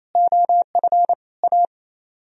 Morse code, heard as a keyed tone, about 35 wpm, 700 Hz. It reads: OF A
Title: Speed-Racing - Sets of 5 Words 28wpm